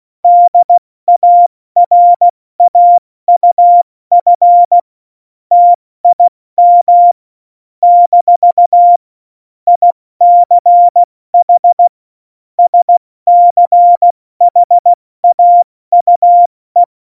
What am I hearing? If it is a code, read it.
Morse code, heard as a keyed tone, about 16 wpm, 700 Hz. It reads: DARAUF TIM - ICH SCHAUE
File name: Morsekurs / Witze / Witz 21. November 2017, 70 F80